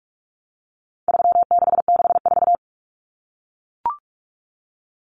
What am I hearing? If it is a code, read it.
Morse code, heard as a keyed tone, about 44 wpm, 700 Hz. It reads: 3664
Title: Common Cut Numbers - 4-Digit 45wpm